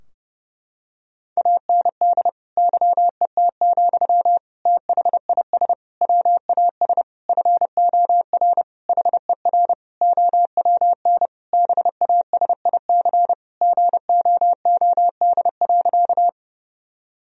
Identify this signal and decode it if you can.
Morse code, heard as a keyed tone, about 30 wpm, 700 Hz. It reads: AND YET, THIS WAS FOR HER OWN BASIC GOOD.
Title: Sentences from Top 1000 Words - Review Part 3 - 30wpm